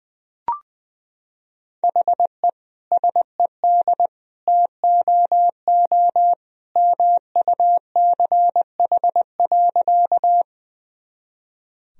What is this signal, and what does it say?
Morse code, heard as a keyed tone, about 21 wpm, 700 Hz. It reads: HE SED TOO MUCH.
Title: Sentences from Top 200 Words and Abbreviations 20wpm